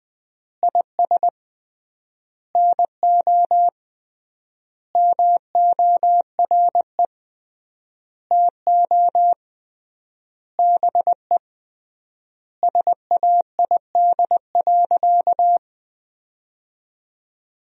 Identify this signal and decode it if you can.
Morse code, heard as a keyed tone, about 20 wpm, 700 Hz. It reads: IS NO MORE TO BE SAID.
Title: 3x Word-Spacing - Sentences from Top 100 Words - 20wpm